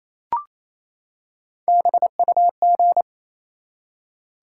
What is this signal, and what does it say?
Morse code, heard as a keyed tone, about 28 wpm, 700 Hz. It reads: BUG
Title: Three Letter Words - 28wpm